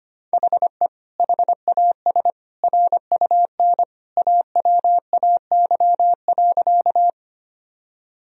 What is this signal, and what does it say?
Morse code, heard as a keyed tone, about 25 wpm, 700 Hz. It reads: HE HAS RUN AWAY.
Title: Sentences from Top 1000 Words - Review Part 1 - 25wpm